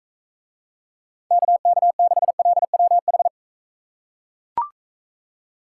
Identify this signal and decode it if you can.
Morse code, heard as a keyed tone, about 40 wpm, 700 Hz. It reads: KK6LWH